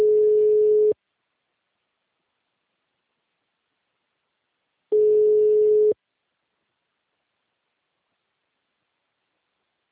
Ringback tone (Europe, 425 Hz).